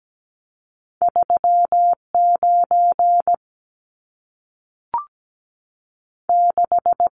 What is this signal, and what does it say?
Morse code, heard as a keyed tone, about 17 wpm, 700 Hz. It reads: 3 9 6
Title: Numbers - 2-Digit 17wpm